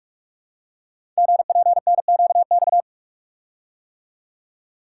Morse code, heard as a keyed tone, about 44 wpm, 700 Hz. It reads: GPNQX
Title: Code Groups - 5 Letter:Numbers - 45wpm